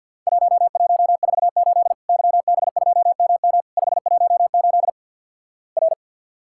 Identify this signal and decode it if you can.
Morse code, heard as a keyed tone, about 48 wpm, 700 Hz. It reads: 1148 YB2MM 518 R
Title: Worked All Europe DX Contest - QTC Exchange - 50wpm